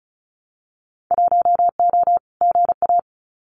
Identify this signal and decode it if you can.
Morse code, heard as a keyed tone, about 34 wpm, 700 Hz. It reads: 1O GA
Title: Winter Field Day Exchanges 35wpm